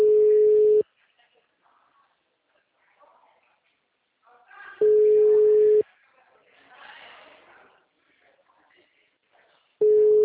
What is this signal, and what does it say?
Ringback tone (Europe, 425 Hz), with speech.